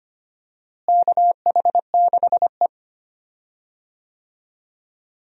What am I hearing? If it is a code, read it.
Morse code, heard as a keyed tone, about 25 wpm, 700 Hz. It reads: KH6E
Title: DXCC-100 Call Signs - 4 Characters - Repeated 2x - 25wpm